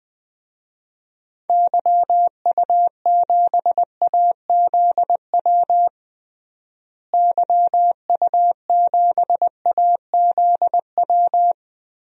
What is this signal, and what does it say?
Morse code, heard as a keyed tone, about 21 wpm, 700 Hz. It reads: YU7AZW YU7AZW
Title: DXCC-100 Call Signs - 6 Characters - Repeated 3x - 20wpm